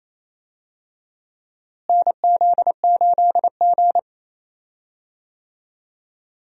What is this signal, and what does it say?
Morse code, heard as a keyed tone, about 28 wpm, 700 Hz. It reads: NZ8G
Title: USA Call Signs - 4 Characters - Repeated 2x - 28wpm